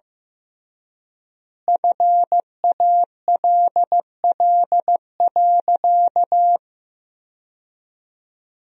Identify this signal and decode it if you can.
Morse code, heard as a keyed tone, about 16 wpm, 700 Hz. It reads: FALL.
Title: 3x Word-Spacing - Sentences from Top 500 Words - 15wpm